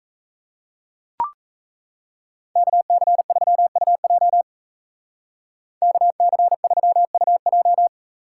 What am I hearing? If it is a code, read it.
Morse code, heard as a keyed tone, about 40 wpm, 700 Hz. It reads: KC3UJ KC3UJ